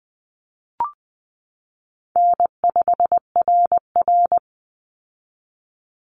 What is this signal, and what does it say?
Morse code, heard as a keyed tone, about 20 wpm, 700 Hz. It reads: N5RR
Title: Easy Call Signs - 20wpm